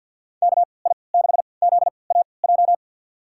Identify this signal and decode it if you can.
Morse code, heard as a keyed tone, about 49 wpm, spaced as for 28 wpm, 700 Hz. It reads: KI6ZAJ
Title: USA Call Signs - 6 Characters - 3x Character Spacing - 50wpm